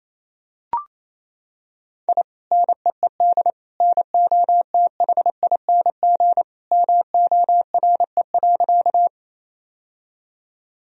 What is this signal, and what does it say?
Morse code, heard as a keyed tone, about 28 wpm, 700 Hz. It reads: I NEED NOTHING MORE.
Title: Sentences from Top 1000 Words - Review Part 1 - 28wpm